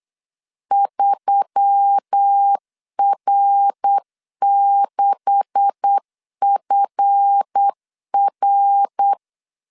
Morse code, heard as a keyed tone, about 8 wpm, 800 Hz. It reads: 3R6FR
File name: CW / DET / 1 9 1500C S C 8wpm 5653